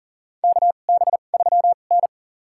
Morse code, heard as a keyed tone, about 40 wpm, spaced as for 29 wpm, 700 Hz. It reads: KB3N
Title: USA Call Signs - 4 Characters - 2x Character Spacing - 40wpm